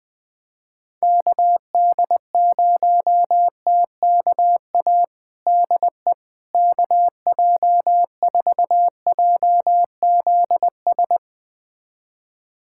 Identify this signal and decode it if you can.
Morse code, heard as a keyed tone, about 21 wpm, 700 Hz. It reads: KD0TKA DE KJ4JZS
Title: USA Callsign DE Callsign - 6 Characters - 20wpm